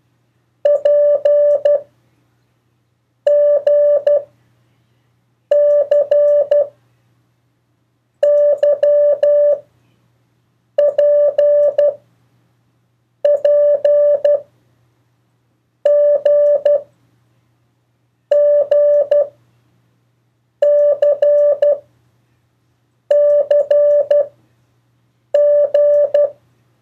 Morse code, heard as a keyed tone, about 12 wpm, 580 Hz. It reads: P G C Y P P G G C C G
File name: 1of4-cw-letters